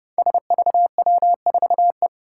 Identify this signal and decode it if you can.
Morse code, heard as a keyed tone, about 30 wpm, 700 Hz. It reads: SVW4E
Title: Five Challenging Letter-Number Combinations 30wpm